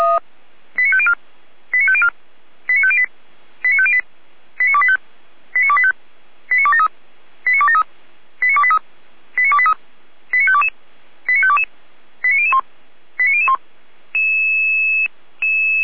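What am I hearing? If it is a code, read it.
Five-tone selective call (ZVEI), heard as quick runs of five tones, tones of about 70 ms. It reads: 89504 89504 89508 89508 89285 89285 89273 89273 89283 89283 89522 89522 89001 89001
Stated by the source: ZVEI five-tone selective call received by SDR on channel 455, recorded by Feuerwehr Orlamünde.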